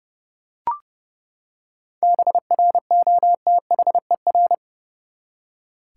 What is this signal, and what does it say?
Morse code, heard as a keyed tone, about 31 wpm, 700 Hz. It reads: BROTHER
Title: Seven Letter Words - 30wpm